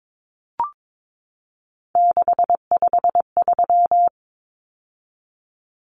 Morse code, heard as a keyed tone, about 22 wpm, 700 Hz. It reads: 653
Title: Numbers - 3-Digit 22wpm